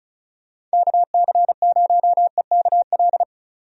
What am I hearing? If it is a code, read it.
Morse code, heard as a keyed tone, about 34 wpm, 700 Hz. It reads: KC0EKL
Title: USA Call Signs - 6 Characters - 35wpm